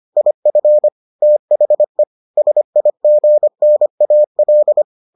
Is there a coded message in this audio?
Yes: Morse code from 0.2 to 4.8 s, heard as a keyed tone, about 25 wpm, 600 Hz. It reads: IF THE SIGNAL